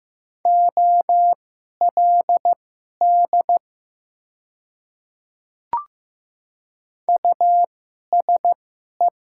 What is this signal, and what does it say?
Morse code, heard as a keyed tone, about 15 wpm, spaced as for 11 wpm, 700 Hz. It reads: OLD USE
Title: 2x Character Spacing - Top 200 Words - 15wpm